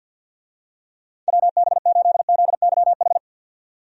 Morse code, heard as a keyed tone, about 46 wpm, 700 Hz. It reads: WB8ZYH
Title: USA Call Signs - 6 Characters - 50wpm